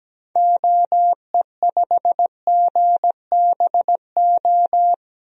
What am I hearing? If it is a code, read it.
Morse code, heard as a keyed tone, about 17 wpm, 700 Hz. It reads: OE5GBO
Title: DXCC-100 Call Signs - 6 Characters - Repeated 2x - 17wpm